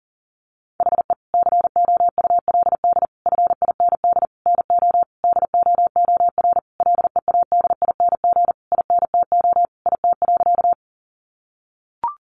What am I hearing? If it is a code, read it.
Morse code, heard as a keyed tone, about 40 wpm, 700 Hz. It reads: HE COULD FIND NO DOOR LEADING INTO IT.
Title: Sentences from Top 1100 Words 40wpm